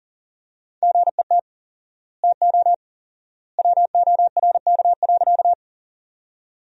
Morse code, heard as a keyed tone, about 40 wpm, 700 Hz. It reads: GET TO WORK.
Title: 4x Word-Spacing - Sentences from Top 200 Words - 40wpm